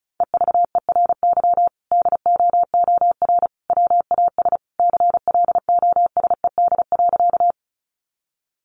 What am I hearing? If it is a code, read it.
Morse code, heard as a keyed tone, about 34 wpm, 700 Hz. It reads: EVERY DOOR WAS CLOSED.
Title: Sentences from Top 1400 Words 35wpm